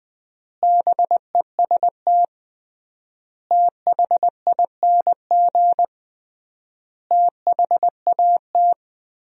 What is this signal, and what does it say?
Morse code, heard as a keyed tone, about 21 wpm, 700 Hz. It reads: BEST THING THAT
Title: 3x Word-Spacing - Sentences from Top 400 Words - 20wpm